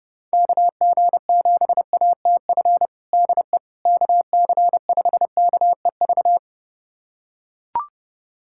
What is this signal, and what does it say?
Morse code, heard as a keyed tone, about 32 wpm, 700 Hz. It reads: KG7ATF DE KC5KEV
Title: USA Callsign DE Callsign - 6 Characters - 30wpm